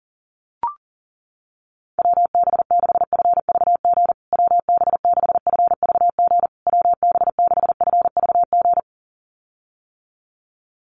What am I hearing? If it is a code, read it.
Morse code, heard as a keyed tone, about 37 wpm, 700 Hz. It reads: WB6FVG WB6FVG WB6FVG
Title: USA Call Signs Repeated 3x 40wpm